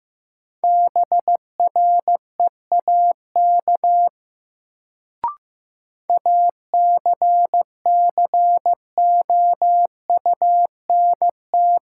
Morse code, heard as a keyed tone, about 15 wpm, 700 Hz. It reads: BREAK ACCOUNT
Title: Sentences from Top 900 Words - New Word Drill - 15wpm